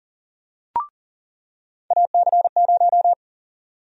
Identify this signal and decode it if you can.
Morse code, heard as a keyed tone, about 40 wpm, 700 Hz. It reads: AC0
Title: USA Call Sign Prefixes - 40wpm